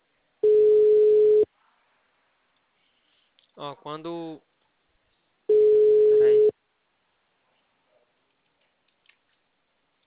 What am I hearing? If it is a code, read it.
Ringback tone (Europe, 425 Hz), with speech.